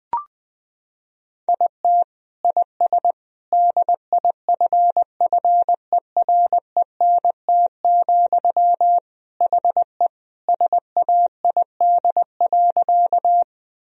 Morse code, heard as a keyed tone, about 20 wpm, 700 Hz. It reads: IT IS DIFFERENT, HE SAID.
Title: Sentences from Top 1000 Words - Review Part 3 - 20wpm